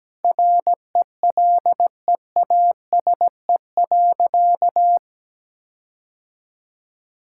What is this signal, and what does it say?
Morse code, heard as a keyed tone, about 17 wpm, 700 Hz. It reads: RELEASE.
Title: Sentences from Top 1600 Words 17wpm